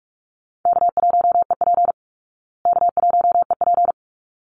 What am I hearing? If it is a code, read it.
Morse code, heard as a keyed tone, about 44 wpm, 700 Hz. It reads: K1EP K1EP
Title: Easy Call Signs Repeated 2x - 45wpm